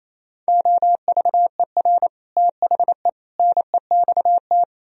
Morse code, heard as a keyed tone, about 38 wpm, 700 Hz. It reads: OVER THE NEXT